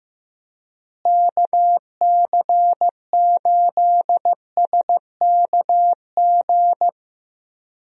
Morse code, heard as a keyed tone, about 15 wpm, 700 Hz. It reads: KC8SKG
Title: USA Call Signs - 6 Characters - Repeated 2x - 15wpm